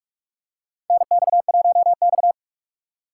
Morse code, heard as a keyed tone, about 46 wpm, 700 Hz. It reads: NX1X
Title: USA Call Signs - 2x Character Spacing - 45wpm